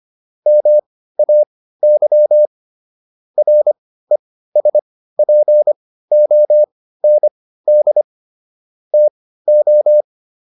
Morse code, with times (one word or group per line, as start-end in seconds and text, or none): 0.5-2.5 MAY
3.4-8.0 RESPOND
8.9-10.0 TO